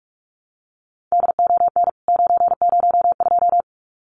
Morse code, heard as a keyed tone, about 44 wpm, 700 Hz. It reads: DON 902
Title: CWT Exchange 45wpm